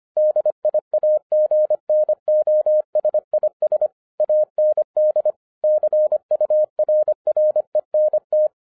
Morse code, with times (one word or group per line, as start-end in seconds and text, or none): 0.2-3.9 DIAGNOSIS
4.2-5.3 AND
5.6-8.5 CURRENT